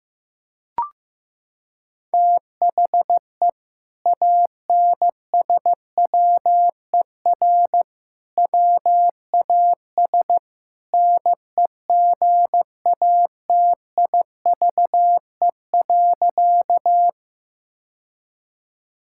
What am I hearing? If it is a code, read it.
Morse code, heard as a keyed tone, about 15 wpm, 700 Hz. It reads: THE ANSWER WAS NEGATIVE.
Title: Sentences from Top 1000 Words - Review Part 6 - 15wpm